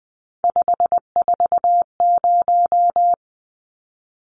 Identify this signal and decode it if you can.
Morse code, heard as a keyed tone, about 20 wpm, 700 Hz. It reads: 540
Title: Numbers - 3-Digit 20wpm